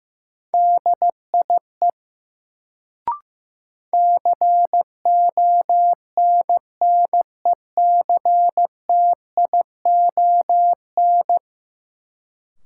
Morse code, heard as a keyed tone, about 15 wpm, 700 Hz. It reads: DIE CONNECTION
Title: Sentences from Top 1300 Words - New Word Drill - 15wpm